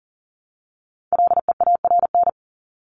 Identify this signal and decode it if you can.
Morse code, heard as a keyed tone, about 40 wpm, 700 Hz. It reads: LEARN
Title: ING Suffix - Top 100 Words with Base Words 40wpm